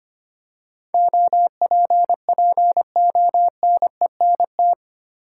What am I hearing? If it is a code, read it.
Morse code, heard as a keyed tone, about 25 wpm, 700 Hz. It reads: OPPONENT
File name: Eight Letter Words - 25wpm